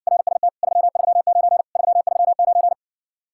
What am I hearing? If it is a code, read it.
Morse code, heard as a keyed tone, about 60 wpm, 700 Hz. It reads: RST 339 339